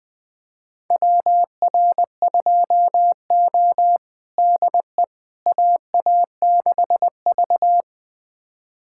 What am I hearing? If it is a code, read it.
Morse code, heard as a keyed tone, about 20 wpm, 700 Hz. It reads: WR2O DE AA6V
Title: USA Callsign DE Callsign - 4 Characters - 20wpm